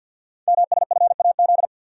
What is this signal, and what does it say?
Morse code, heard as a keyed tone, about 48 wpm, 690 Hz. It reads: MSFAZ
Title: Code Groups - 5 Letters - 50wpm